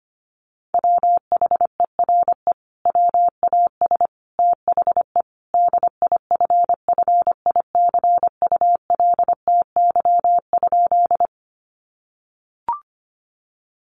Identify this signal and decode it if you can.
Morse code, heard as a keyed tone, about 25 wpm, 700 Hz. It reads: WHERE WAS THE DIFFICULTY?
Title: Sentences from Top 1500 Words 25wpm